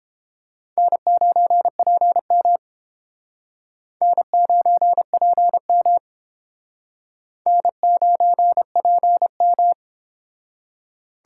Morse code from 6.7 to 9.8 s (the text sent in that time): N9PM